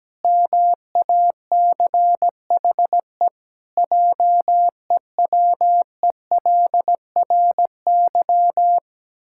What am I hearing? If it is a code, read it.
Morse code, heard as a keyed tone, about 17 wpm, 700 Hz. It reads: MACHE JEWELRY
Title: Pangram Challenge v2 - 17wpm